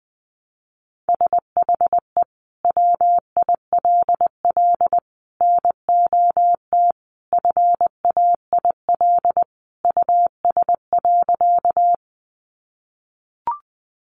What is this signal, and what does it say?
Morse code, heard as a keyed tone, about 20 wpm, 700 Hz. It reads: SHE WILL NOT FAIL US.
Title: Sentences from Top 1100 Words 20wpm